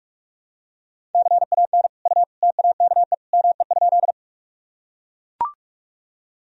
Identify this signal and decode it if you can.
Morse code, heard as a keyed tone, about 41 wpm, 700 Hz. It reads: CAN U TAKE ME?
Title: Sentences from Top 200 Words and Abbreviations 45wpm